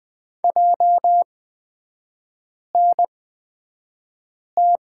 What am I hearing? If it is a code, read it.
Morse code, heard as a keyed tone, about 20 wpm, spaced as for 5 wpm, 700 Hz. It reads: JNT
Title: 243 Lesson 39e - Letter z - Three Characters 20wpm